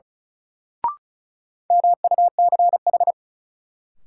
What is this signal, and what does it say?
Morse code, heard as a keyed tone, about 36 wpm, 700 Hz. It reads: MUCH
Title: Sentences from Top 200 Words - New Word Drill - 35wpm